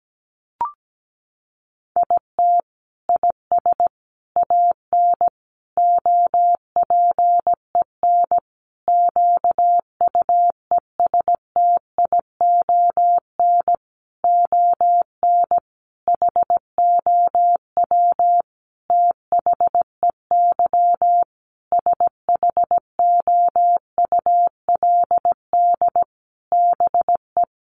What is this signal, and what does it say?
Morse code, heard as a keyed tone, about 17 wpm, 700 Hz. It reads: IT IS AN OPEN QUESTION ON HOW THEY SHOULD BE
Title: Sentences from Top 1300 Words 17wpm